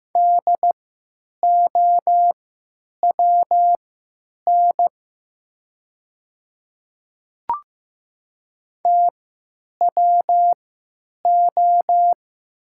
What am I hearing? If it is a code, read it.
Morse code, heard as a keyed tone, about 15 wpm, spaced as for 9 wpm, 700 Hz. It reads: DOWN TWO
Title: 3x Character Spacing - Top 100 Words - 15wpm